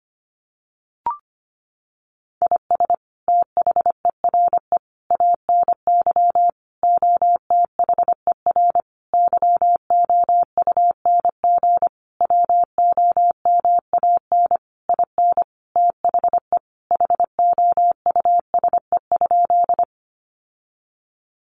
Morse code, heard as a keyed tone, about 25 wpm, 700 Hz. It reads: IS THERE ANY OTHER YOUNG WOMAN IN THE HOUSE?
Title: Sentences from Top 1100 Words 25wpm